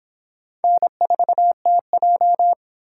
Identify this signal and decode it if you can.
Morse code, heard as a keyed tone, about 23 wpm, 700 Hz. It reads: N4TJ